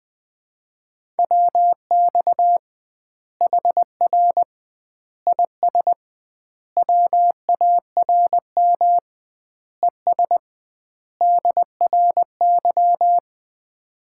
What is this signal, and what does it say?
Morse code, heard as a keyed tone, about 20 wpm, 700 Hz. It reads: WX HR IS WARM ES DRY
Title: Common QSO Phrases - 2x Word-Spacing - 20wpm